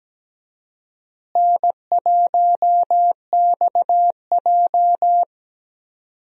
Morse code, heard as a keyed tone, about 17 wpm, 700 Hz. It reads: N1XJ
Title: USA Call Signs - 4 Characters - Repeated 2x - 17wpm